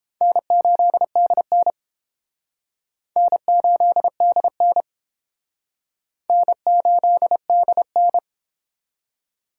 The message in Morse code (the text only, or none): N8DN N8DN N8DN